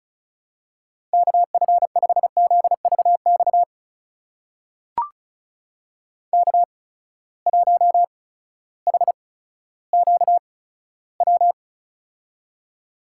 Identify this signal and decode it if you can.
Morse code, heard as a keyed tone, about 36 wpm, spaced as for 16 wpm, 700 Hz. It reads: KF5ZVX K1HQW
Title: USA Call Signs - 8x Character Spacing - 35wpm